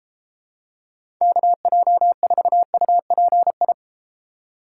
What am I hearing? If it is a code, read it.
Morse code, heard as a keyed tone, about 32 wpm, 700 Hz. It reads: KJ4UPI